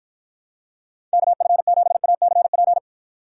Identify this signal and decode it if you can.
Morse code, heard as a keyed tone, about 53 wpm, 700 Hz. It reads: KF7ACP